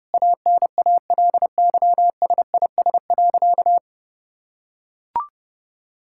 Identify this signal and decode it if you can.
Morse code, heard as a keyed tone, about 30 wpm, 700 Hz. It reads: ANALYSIS.
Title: Sentences from Top 1000 Words - Review Part 5 - 30wpm